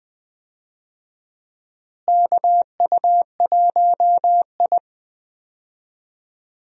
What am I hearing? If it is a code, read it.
Morse code, heard as a keyed tone, about 20 wpm, 700 Hz. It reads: KU1I
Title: USA Call Signs - 4 Characters - Repeated 2x - 20wpm